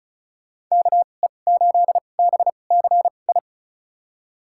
Morse code, heard as a keyed tone, about 36 wpm, spaced as for 25 wpm, 700 Hz. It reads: KE8BCI
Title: USA Call Signs - 6 Characters - 2x Character Spacing - 35wpm